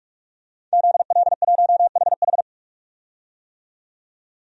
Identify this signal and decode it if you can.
Morse code, heard as a keyed tone, about 47 wpm, 700 Hz. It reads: ZL1HH
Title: DXCC-100 Call Signs - 5 Characters - 45wpm